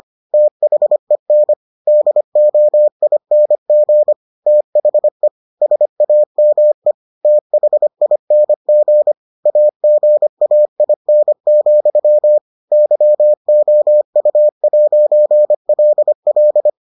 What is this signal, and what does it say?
Morse code, heard as a keyed tone, about 25 wpm, 600 Hz. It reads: THEN DOING THE SAME THING AGAIN, YOU'LL